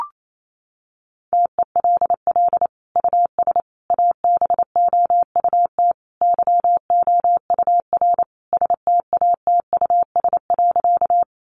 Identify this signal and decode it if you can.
Morse code, heard as a keyed tone, about 28 wpm, 700 Hz. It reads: TELL US ABOUT YOUR STATUS.
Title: Sentences from Top 1400 Words 28wpm